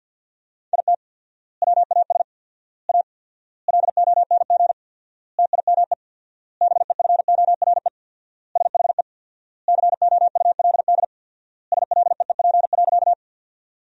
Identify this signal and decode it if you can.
Morse code, heard as a keyed tone, about 48 wpm, 700 Hz. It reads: IT WAS A LONG TIME BEFORE SHE COULD SLEEP.
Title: 4x Word-Spacing - Sentences from Top 500 Words - 50wpm